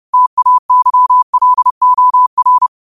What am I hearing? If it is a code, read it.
Morse code, heard as a keyed tone, about 30 wpm, 1000 Hz. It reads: TAYLOR